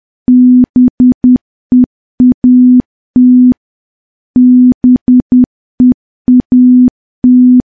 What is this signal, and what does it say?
Morse code, heard as a keyed tone, about 10 wpm, 250 Hz. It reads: BEAT BEAT
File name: morse / morse-training-10WPM